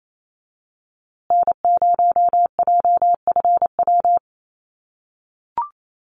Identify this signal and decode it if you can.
Morse code, heard as a keyed tone, about 28 wpm, 700 Hz. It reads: N0JFW
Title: USA Call Signs Repeated 2x 28wpm